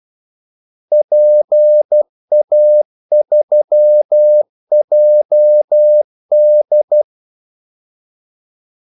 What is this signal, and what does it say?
Morse code, heard as a keyed tone, about 12 wpm, 600 Hz. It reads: PA3JD
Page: 2 ekstra ordmellemrum mellem kaldesignalerne